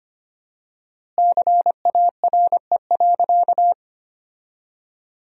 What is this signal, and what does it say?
Morse code, heard as a keyed tone, about 25 wpm, 700 Hz. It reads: CARE.
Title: 4x Word-Spacing - Sentences from Top 300 Words - 25wpm